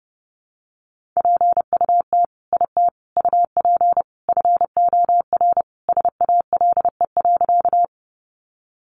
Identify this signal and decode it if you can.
Morse code, heard as a keyed tone, about 28 wpm, 700 Hz. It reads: PUT IT UP FOR SALE.
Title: Sentences from Top 1400 Words 30wpm